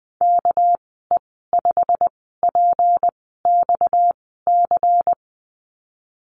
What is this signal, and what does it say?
Morse code, heard as a keyed tone, about 20 wpm, spaced as for 15 wpm, 700 Hz. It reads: KE5PXC
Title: 245 Lesson 39g - Letter z - Call Signs 20wpm